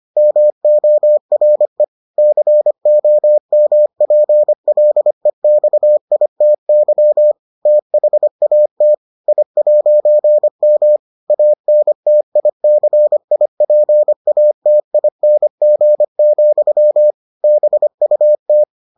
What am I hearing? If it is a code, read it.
Morse code, heard as a keyed tone, about 25 wpm, 600 Hz. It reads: MORE COMPLEXITY THAT I'M ANTICIPATING, BUT